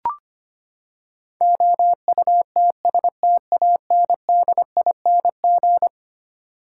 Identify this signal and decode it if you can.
Morse code, heard as a keyed tone, about 25 wpm, 700 Hz. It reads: OUTSTANDING
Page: morse-code-ninja-practice-25wpm